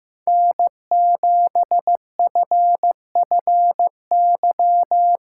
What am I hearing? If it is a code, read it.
Morse code, heard as a keyed tone, about 15 wpm, 700 Hz. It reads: N7FFY
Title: USA Call Signs - 2x Character Spacing - 15wpm